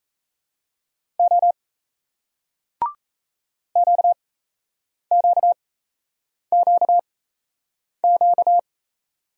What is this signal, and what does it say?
Morse code, heard as a keyed tone, about 37 wpm, spaced as for 8 wpm, 700 Hz. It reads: O QQQQ